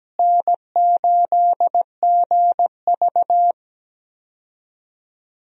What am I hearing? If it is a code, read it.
Morse code, heard as a keyed tone, about 17 wpm, 700 Hz. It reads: N8GV
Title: USA Call Signs - 4 Characters - Repeated 2x - 17wpm